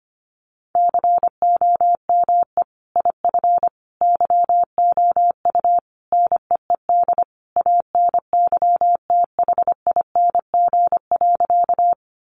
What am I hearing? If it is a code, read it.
Morse code, heard as a keyed tone, about 25 wpm, 700 Hz. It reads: COME IF YOU NEED ANYTHING.
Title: Sentences from Top 1000 Words 25wpm